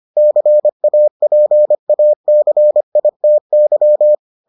Morse code, heard as a keyed tone, about 25 wpm, 600 Hz. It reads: CAPACITY